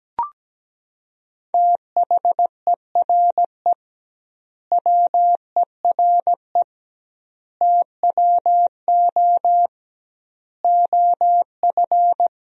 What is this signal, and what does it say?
Morse code, heard as a keyed tone, about 17 wpm, 700 Hz. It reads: THERE WERE TWO OF
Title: 2x Word-Spacing - Sentences from Top 100 Words - 17wpm